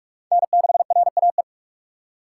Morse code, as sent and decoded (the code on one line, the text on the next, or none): -. -.... .-. .- .
N6RAE